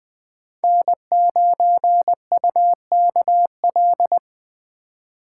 Morse code, heard as a keyed tone, about 20 wpm, 700 Hz. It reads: N9UKL
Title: Speed Racing - USA 1x3 Call Signs 17wpm